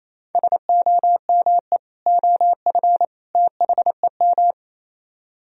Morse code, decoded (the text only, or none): SOME OF THEM